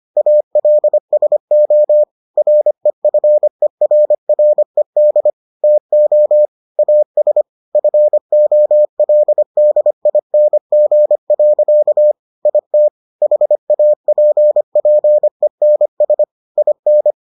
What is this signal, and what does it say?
Morse code, heard as a keyed tone, about 25 wpm, 600 Hz. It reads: ALSO REFERRED TO AS FOLDING. IT HAPPENS IN